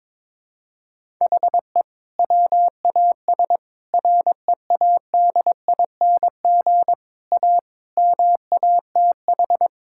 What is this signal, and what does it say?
Morse code, heard as a keyed tone, about 22 wpm, 700 Hz. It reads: HE WAS READING A MATH
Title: Sentences from Top 1000 Words - Review Part 6 - 22wpm